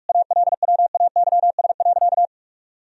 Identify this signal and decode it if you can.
Morse code, heard as a keyed tone, about 41 wpm, 700 Hz. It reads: ALWAYS.
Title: Sentences from Top 1000 Words - Review Part 4 - 45wpm